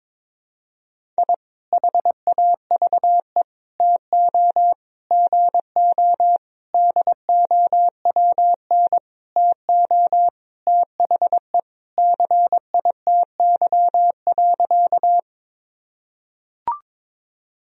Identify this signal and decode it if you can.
Morse code, heard as a keyed tone, about 22 wpm, 700 Hz. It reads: I HAVE TO GO DOWN TO THE CITY.
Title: Sentences from Top 1000 Words - Review Part 1 - 22wpm